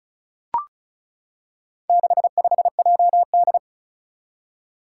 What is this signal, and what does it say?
Morse code, heard as a keyed tone, about 34 wpm, 700 Hz. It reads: 65JD
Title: Four Challenging Letter-Number Combinations 35wpm